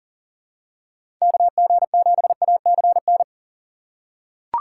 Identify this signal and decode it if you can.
Morse code, heard as a keyed tone, about 40 wpm, 700 Hz. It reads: KG7ACN
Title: USA Call Signs - 6 Characters - 6x Character Spacing - 40wpm